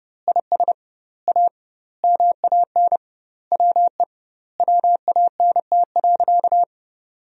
Morse code, heard as a keyed tone, about 29 wpm, 700 Hz. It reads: IS A MAN WE WANT.
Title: 2x Word-Spacing - Sentences from Top 200 Words - 30wpm